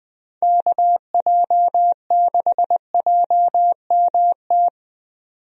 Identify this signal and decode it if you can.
Morse code, heard as a keyed tone, about 21 wpm, 700 Hz. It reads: KJ6JMT
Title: USA Call Signs - 6 Characters - 20wpm